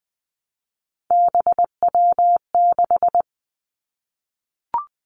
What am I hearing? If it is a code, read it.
Morse code, heard as a keyed tone, about 20 wpm, 700 Hz. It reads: BW6
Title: Three Challenging Letter-Number Combinations 20wpm